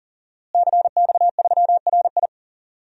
Morse code, heard as a keyed tone, about 37 wpm, 700 Hz. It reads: CX3RI
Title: DXCC-100 Call Signs - 5 Characters - Repeated 3x - 40wpm